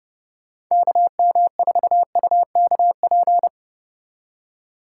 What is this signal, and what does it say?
Morse code, heard as a keyed tone, about 30 wpm, 700 Hz. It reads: KM4UKP